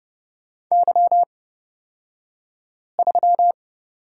Morse code, heard as a keyed tone, about 31 wpm, spaced as for 5 wpm, 700 Hz. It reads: Y3
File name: Lesson 37d - Letter x - Two Characters 30wpm